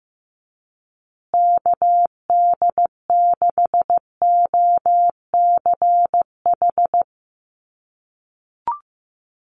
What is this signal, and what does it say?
Morse code, heard as a keyed tone, about 15 wpm, 700 Hz. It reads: KD6OCH
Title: USA Call Signs Repeated 3x 15wpm